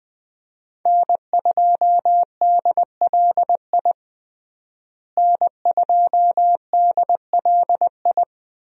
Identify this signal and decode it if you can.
Morse code, heard as a keyed tone, about 21 wpm, 700 Hz. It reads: N2DLI N2DLI
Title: USA Call Signs - 5 Characters - Repeated 2x - 20wpm